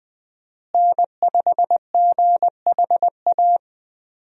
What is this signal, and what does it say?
Morse code, heard as a keyed tone, about 20 wpm, 700 Hz. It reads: N5GHA
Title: USA Call Signs - 5 Characters - Repeated 2x - 20wpm